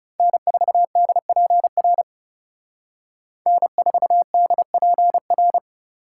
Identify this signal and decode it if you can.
Morse code, heard as a keyed tone, about 34 wpm, 700 Hz. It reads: N4DPR N4DPR
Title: Speed Racing - USA 1x3 Call Signs 30wpm